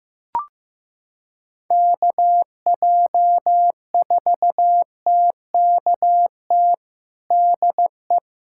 Morse code, heard as a keyed tone, about 15 wpm, 700 Hz. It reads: KJ4TKT DE
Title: USA Callsign DE Callsign 15wpm